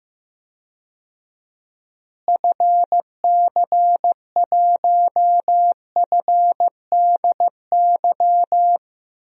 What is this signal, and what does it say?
Morse code, heard as a keyed tone, about 15 wpm, 700 Hz. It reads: FC1FDY
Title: DXCC-100 Call Signs - 6 Characters - Repeated 3x - 15wpm